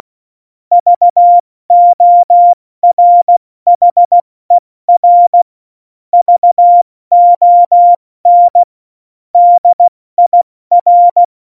Morse code, heard as a keyed tone, about 16 wpm, 700 Hz. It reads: VORHER VON DIR
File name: Morsekurs / Witze / Witz 25. Juni 2019, 70 F80